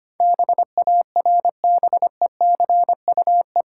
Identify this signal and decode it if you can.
Morse code, heard as a keyed tone, about 26 wpm, 700 Hz. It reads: BARBECUE
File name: Easy Words - Speed-Racing with Hints - 25wpm